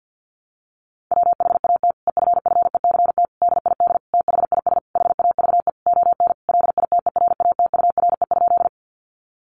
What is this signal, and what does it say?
Morse code, heard as a keyed tone, about 48 wpm, 700 Hz. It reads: WHAT EFFECT DID THIS HAVE ON LITERATURE?
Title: Sentences from Top 1000 Words 50wpm